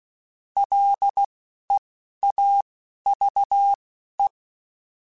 Morse code, heard as a keyed tone, about 16 wpm, spaced as for 11 wpm, 800 Hz. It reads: LEAVE
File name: Spaced / Chuckles Spaced-mp3s / Temptation sp @15